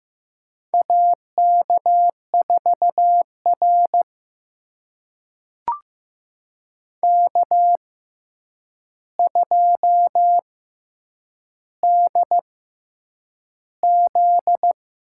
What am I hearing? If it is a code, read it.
Morse code, heard as a keyed tone, about 15 wpm, spaced as for 8 wpm, 700 Hz. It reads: AK4R K2DZ
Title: USA Call Signs - 4 Characters - 6x Character Spacing - 15wpm